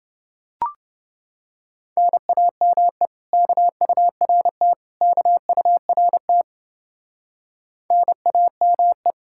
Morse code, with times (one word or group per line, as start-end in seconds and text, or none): 2.0-3.1 NAME
3.3-4.7 KURT
5.0-6.4 KURT
7.9-9.1 NAME